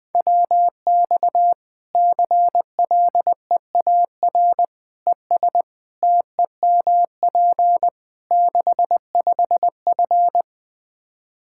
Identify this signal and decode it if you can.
Morse code, heard as a keyed tone, about 20 wpm, 700 Hz. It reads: WX CLEAR ES TEMP 65F
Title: Practice QSO - Exchange 2 - Info - 20wpm